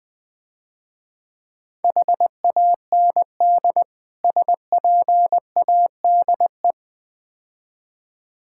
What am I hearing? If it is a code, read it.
Morse code, heard as a keyed tone, about 20 wpm, 700 Hz. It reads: HAND SPADE
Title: Easy Words - Speed-Racing with Hints - 20wpm